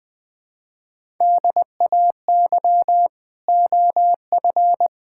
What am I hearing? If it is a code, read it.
Morse code, heard as a keyed tone, about 20 wpm, 700 Hz. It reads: DAY OF
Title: Sets of 9 Words - 20wpm